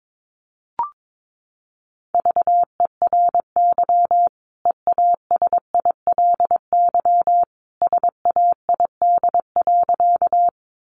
Morse code, heard as a keyed tone, about 22 wpm, 700 Hz. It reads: VERY EASILY SAID.
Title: Sentences from Top 900 Words 22wpm